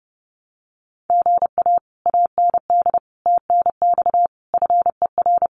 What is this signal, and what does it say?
Morse code, heard as a keyed tone, about 30 wpm, 700 Hz. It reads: GA AND TNX FER
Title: Common QSO Phrases Encore 30wpm